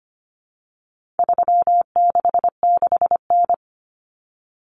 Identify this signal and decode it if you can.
Morse code, heard as a keyed tone, about 25 wpm, 700 Hz. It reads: 366N
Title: Common Cut Numbers - 4-Digit 25wpm